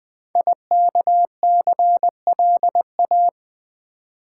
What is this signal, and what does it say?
Morse code, heard as a keyed tone, about 20 wpm, 700 Hz. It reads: IKCLA
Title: Code Groups - 5 Letter:Numbers - 20wpm